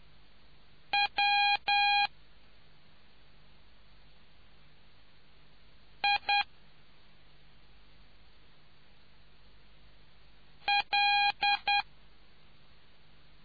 Morse code, heard as a keyed tone, about 10 wpm, 810 Hz. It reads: W I L